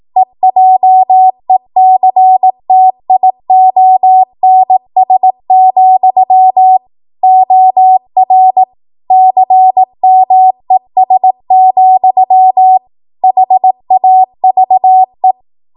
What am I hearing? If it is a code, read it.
Morse code, heard as a keyed tone, about 18 wpm, 750 Hz. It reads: EJECTIONS, OR CMES, HAVE
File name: Morse / 18WPM